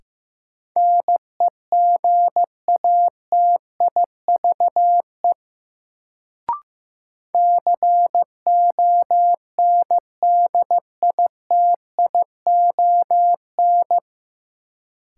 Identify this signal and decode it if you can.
Morse code, heard as a keyed tone, about 15 wpm, 700 Hz. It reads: NEGATIVE CONDITION
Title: Sentences from Top 1000 Words - New Word Drill - 15wpm